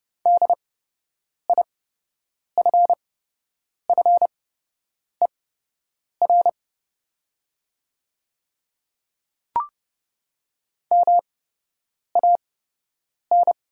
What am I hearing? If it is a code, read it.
Morse code, heard as a keyed tone, about 30 wpm, spaced as for 8 wpm, 700 Hz. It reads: DIFFER MAN
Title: 8x Character Spacing - Top 200 Words - 30wpm